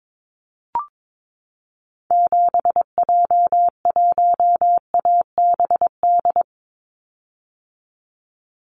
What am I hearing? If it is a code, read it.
Morse code, heard as a keyed tone, about 22 wpm, 700 Hz. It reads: 7J1ABD
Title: Call Signs - 22wpm